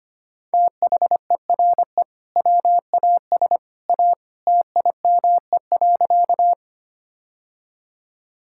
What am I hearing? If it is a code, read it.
Morse code, heard as a keyed tone, about 25 wpm, 700 Hz. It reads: THERE WAS A TIME.
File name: practice / 25wpm / Sentences with 2-Word Phrases - 25wpm